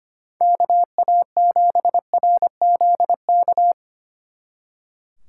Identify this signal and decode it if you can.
Morse code, heard as a keyed tone, about 30 wpm, 700 Hz. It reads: KA7RZK
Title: Speed Racing - USA 2x3 Call Signs 25wpm